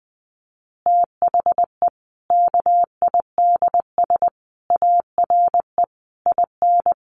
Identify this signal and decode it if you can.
Morse code, heard as a keyed tone, about 20 wpm, 700 Hz. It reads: THE KIDS ARE IN
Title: Easy Phrases 20wpm